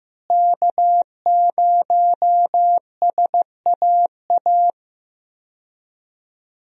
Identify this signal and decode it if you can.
Morse code, heard as a keyed tone, about 15 wpm, 700 Hz. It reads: K0SAA
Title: USA Call Signs - 5 Characters - Repeated 3x - 15wpm